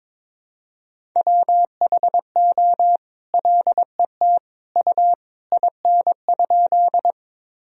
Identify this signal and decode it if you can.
Morse code, heard as a keyed tone, about 22 wpm, 700 Hz. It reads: WHO LET U IN?
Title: Sentences from Top 300 Words and Abbreviations 22wpm